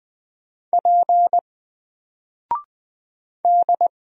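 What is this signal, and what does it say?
Morse code, heard as a keyed tone, about 20 wpm, 700 Hz. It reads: P D
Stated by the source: Single letter and number Morse drill at 20 wpm, from N4LCD.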